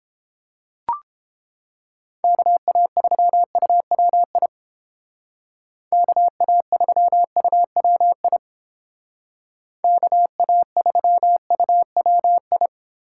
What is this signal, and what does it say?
Morse code, heard as a keyed tone, about 29 wpm, 700 Hz. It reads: KA3UWI KA3UWI KA3UWI